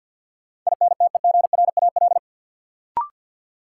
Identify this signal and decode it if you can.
Morse code, heard as a keyed tone, about 52 wpm, 700 Hz. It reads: INTEGRAL
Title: Sets of 5 Words - New Word Drill - 50wpm